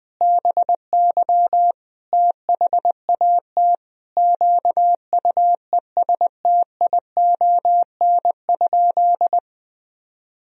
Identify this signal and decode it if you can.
Morse code, heard as a keyed tone, about 20 wpm, 700 Hz. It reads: BY THAT QUESTION?
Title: Sentences from Top 400 Words and Abbreviations 20wpm